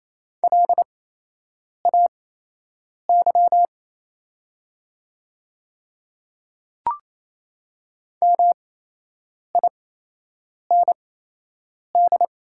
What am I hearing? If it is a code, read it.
Morse code, heard as a keyed tone, about 28 wpm, spaced as for 8 wpm, 700 Hz. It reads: LAY MIND
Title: 8x Character Spacing - Top 500 Words - 28wpm